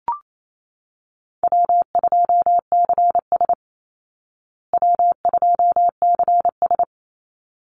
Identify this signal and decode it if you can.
Morse code, heard as a keyed tone, about 28 wpm, 700 Hz. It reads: W2CS W2CS
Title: Easy Call Signs Repeated 2x - 28wpm